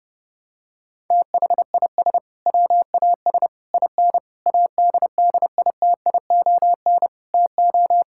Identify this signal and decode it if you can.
Morse code, heard as a keyed tone, about 32 wpm, 700 Hz. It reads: THIS WAS IN ADDITION TO
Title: Sentences from Top 1000 Words - Review Part 6 - 30wpm